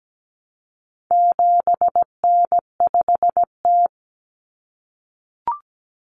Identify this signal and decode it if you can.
Morse code, heard as a keyed tone, about 17 wpm, 700 Hz. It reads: 7N5T
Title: Common Cut Numbers - 4-Digit 17wpm